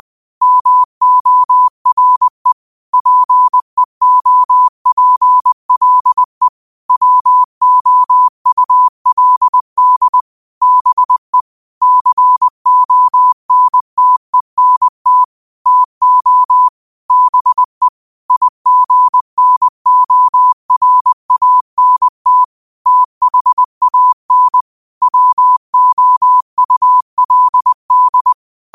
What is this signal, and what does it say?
Morse code, heard as a keyed tone, about 20 wpm, 1000 Hz. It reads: MORE PEOPLE WOULD BE CONTENT TO BE IGNORANT THAN WOULD
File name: qotdfiles / 20wpm